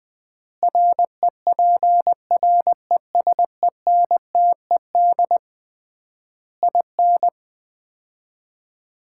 Morse code, heard as a keyed tone, about 19 wpm, 700 Hz. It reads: REPRESENTED IN
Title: 3x Word-Spacing - Sets of 5 Words - 20wpm